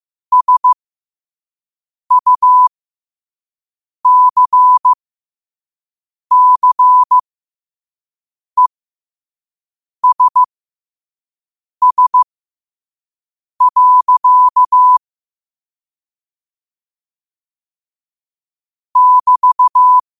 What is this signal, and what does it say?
Morse code, heard as a keyed tone, about 15 wpm, spaced as for 5 wpm, 1000 Hz. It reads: SUCCESS. <BT>
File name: qotdfiles / 5wpm